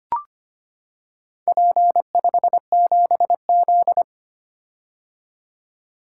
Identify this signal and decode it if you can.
Morse code, heard as a keyed tone, about 25 wpm, 700 Hz. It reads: P57Z
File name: practice / 25wpm / Four Letter-Number Combinations - 25wpm